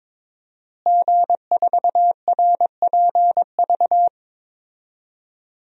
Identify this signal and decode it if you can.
Morse code, heard as a keyed tone, about 22 wpm, 700 Hz. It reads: G4RPV
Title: DXCC-100 Call Signs - 5 Characters - Repeated 3x - 22wpm